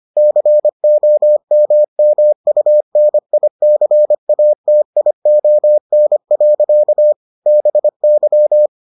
Morse code, heard as a keyed tone, about 25 wpm, 600 Hz. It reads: COMMUNICATION. BY